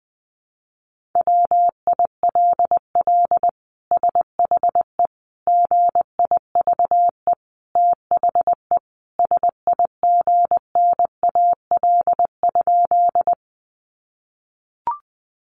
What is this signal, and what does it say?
Morse code, heard as a keyed tone, about 20 wpm, 700 Hz. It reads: WILL SHE GIVE THE SIGNAL?
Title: Sentences from Top 1400 Words 20wpm